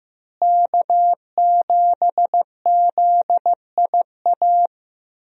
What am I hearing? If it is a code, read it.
Morse code, heard as a keyed tone, about 15 wpm, 700 Hz. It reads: K7ZIA
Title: USA Call Signs - 5 Characters - Repeated 2x - 15wpm